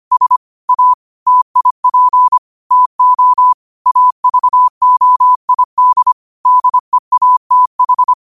Morse code, heard as a keyed tone, about 25 wpm, 1000 Hz. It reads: S A TIP TO AVOID DEATH
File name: qotdfiles / 25wpm